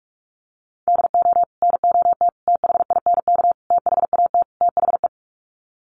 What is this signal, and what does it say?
Morse code, heard as a keyed tone, about 44 wpm, 700 Hz. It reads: DO NOT THINK THAT THE
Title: Sets of 5 Words Encore2 45wpm